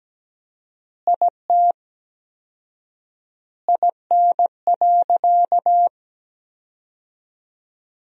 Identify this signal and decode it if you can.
Morse code, heard as a keyed tone, about 17 wpm, 700 Hz. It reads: IT IN.
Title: 4x Word-Spacing - Sentences from Top 200 Words - 17wpm